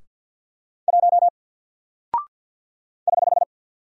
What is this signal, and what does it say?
Morse code, heard as a keyed tone, about 50 wpm, 700 Hz. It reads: 1 <HH>
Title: Single Character (Letter:Number:Punctuation:Prosign) Alt 50wpm